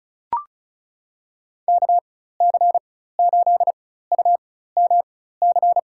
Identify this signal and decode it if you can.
Morse code, heard as a keyed tone, about 36 wpm, spaced as for 16 wpm, 700 Hz. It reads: KC8UMC
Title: USA Call Signs - 6 Characters - 4x Character Spacing - 35wpm